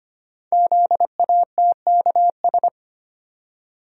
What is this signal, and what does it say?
Morse code, heard as a keyed tone, about 25 wpm, 700 Hz. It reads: ZATKS
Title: Code Groups - 5 Letters - 25wpm